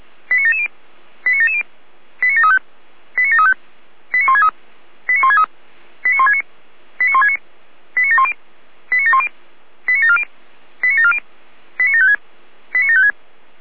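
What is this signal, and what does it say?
Five-tone selective call (ZVEI), heard as quick runs of five tones, tones of about 70 ms. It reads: 89770 89770 89735 89735 89163 89163 89169 89169 89710 89710 89740 89740 89756 89756
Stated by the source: Received by SDR, ZVEI five-tone call.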